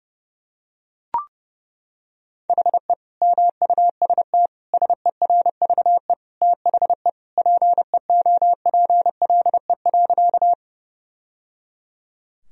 Morse code, heard as a keyed tone, about 30 wpm, 700 Hz. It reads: HE MUST SERVE THE PEOPLE.
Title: Sentences from Top 1000 Words - Review Part 2 - 30wpm